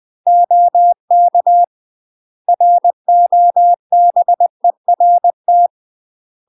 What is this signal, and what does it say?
Morse code, heard as a keyed tone, about 20 wpm, 700 Hz. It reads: OK ROBERT
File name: UK MASTER QSOs TRAINING / 20WPM / UK MASTER QSOs 0006